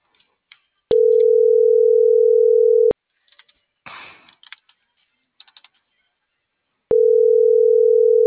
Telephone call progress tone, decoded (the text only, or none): ringback tone